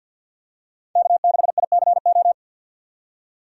Morse code, heard as a keyed tone, about 47 wpm, 700 Hz. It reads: K6ICO
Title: USA Call Signs - 5 Characters - 4x Character Spacing - 50wpm